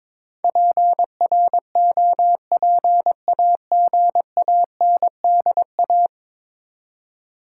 Morse code, heard as a keyed tone, about 21 wpm, 700 Hz. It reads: PROPAGANDA
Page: morse-code-ninja-practice-22wpm